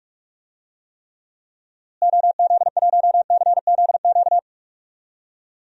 Morse code, heard as a keyed tone, about 41 wpm, 700 Hz. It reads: OZ1CZQ
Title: DXCC-100 Call Signs - 6 Characters - Repeated 2x - 45wpm